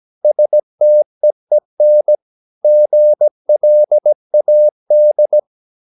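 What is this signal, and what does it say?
Morse code, heard as a keyed tone, about 17 wpm, 600 Hz. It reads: STEEN GLAD
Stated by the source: Slow (QRS) Morse practice at 17 wpm, from CW med Gnister.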